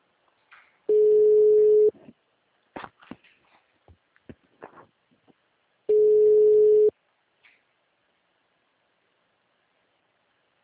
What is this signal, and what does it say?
Ringback tone (Europe, 425 Hz).